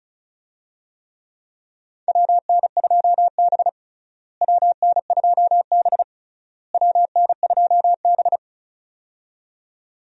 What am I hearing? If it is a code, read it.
Morse code, heard as a keyed tone, about 36 wpm, 700 Hz. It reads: WN2B WN2B WN2B
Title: USA Call Signs - 4 Characters - Repeated 3x - 35wpm